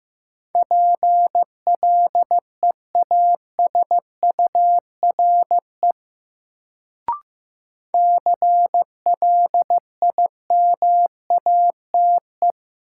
Morse code, heard as a keyed tone, about 15 wpm, 700 Hz. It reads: PLEASURE CLIMATE
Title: Sentences from Top 1600 Words - New Word Drill Alt - 15wpm